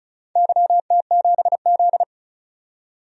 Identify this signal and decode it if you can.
Morse code, heard as a keyed tone, about 36 wpm, 700 Hz. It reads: YT7Z
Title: DXCC-100 Call Signs - 4 Characters - 35wpm